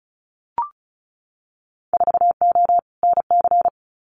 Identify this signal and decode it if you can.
Morse code, heard as a keyed tone, about 34 wpm, 700 Hz. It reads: 4O NC
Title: Winter Field Day Exchanges Encore 35wpm